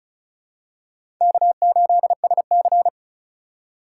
Morse code, heard as a keyed tone, about 34 wpm, 700 Hz. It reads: K8SC
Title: USA Call Signs - 4 Characters - 2x Character Spacing - 35wpm